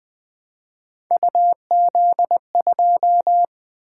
Morse code, heard as a keyed tone, about 20 wpm, 700 Hz. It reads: UZ2
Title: Three Letter-Number Combinations - 20wpm